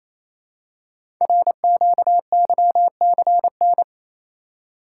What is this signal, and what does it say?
Morse code, heard as a keyed tone, about 29 wpm, 700 Hz. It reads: RQYCN
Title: Code Groups - 5 Letters - 28wpm